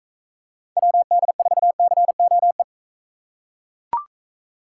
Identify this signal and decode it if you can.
Morse code, heard as a keyed tone, about 38 wpm, 700 Hz. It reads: WD4COE